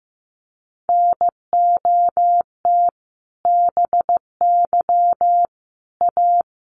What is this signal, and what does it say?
Morse code, heard as a keyed tone, about 15 wpm, 700 Hz. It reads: NOT BY A
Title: Sentences from Top 900 Words 15wpm